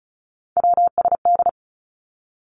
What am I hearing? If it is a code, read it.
Morse code, heard as a keyed tone, about 36 wpm, 700 Hz. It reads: WSD
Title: Three Challenging Letter-Number Combinations 35wpm